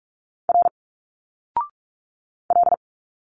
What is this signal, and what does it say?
Morse code, heard as a keyed tone, about 46 wpm, 700 Hz. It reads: R L
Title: Single Letters 45wpm